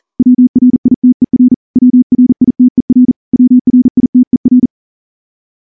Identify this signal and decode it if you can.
Morse code, heard as a keyed tone, about 38 wpm, 250 Hz. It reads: WRITER WRITER WRITER
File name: morse-training-40WPM